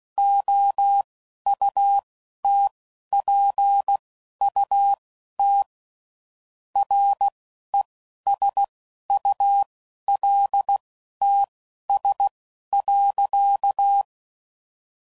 Morse code, heard as a keyed tone, about 16 wpm, spaced as for 11 wpm, 800 Hz. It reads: OUTPUT RESULTS.
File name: CW Code Groups @10WPM / Technical Subjects / Gates Intro